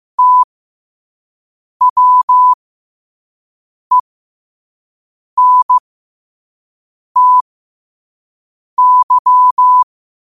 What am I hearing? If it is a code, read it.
Morse code, heard as a keyed tone, about 15 wpm, spaced as for 5 wpm, 1000 Hz. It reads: TWENTY
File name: qotdfiles / 5wpm